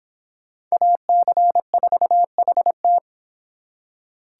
Morse code, heard as a keyed tone, about 25 wpm, 700 Hz. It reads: AC4HT